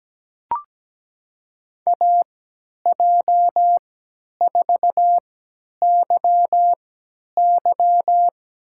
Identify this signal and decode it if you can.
Morse code, heard as a keyed tone, about 17 wpm, spaced as for 10 wpm, 700 Hz. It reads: AJ4YY
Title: USA Call Signs - 5 Characters - 3x Character Spacing - 17wpm